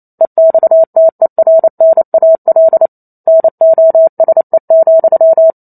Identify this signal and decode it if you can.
Morse code, heard as a keyed tone, about 28 wpm, 650 Hz. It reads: EXTERNAL NOSE,
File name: Chapter 2 - 28 WPM